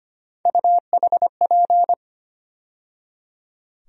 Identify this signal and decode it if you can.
Morse code, heard as a keyed tone, about 25 wpm, 700 Hz. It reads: UHP
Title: USA Call Sign Postfixes - 3 Letters 25wpm